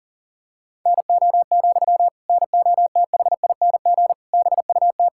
Morse code, heard as a keyed tone, about 40 wpm, 700 Hz. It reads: NO, NOTHING BUT
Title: Sentences from Top 1600 Words 40wpm